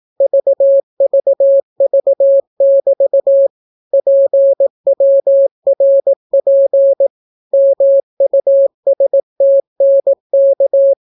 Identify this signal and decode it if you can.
Morse code, heard as a keyed tone, about 18 wpm, 550 Hz. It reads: VVV<BT> PWRP MUSTNK